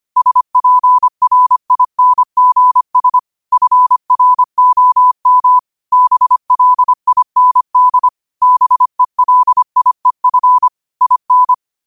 Morse code, heard as a keyed tone, about 25 wpm, 1000 Hz. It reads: SPRINGS FROM BLIND BELIEF IN
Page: New quotes every day in morse code at 25 Words per minute.